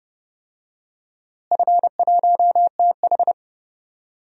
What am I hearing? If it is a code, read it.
Morse code, heard as a keyed tone, about 30 wpm, 700 Hz. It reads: F1TH
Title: DXCC-100 Call Signs - 4 Characters - Repeated 3x - 30wpm